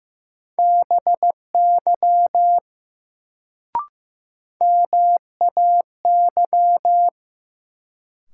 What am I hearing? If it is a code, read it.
Morse code, heard as a keyed tone, about 15 wpm, 700 Hz. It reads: BY MAY
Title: Sets of 2 Words - New Word Drill - 15wpm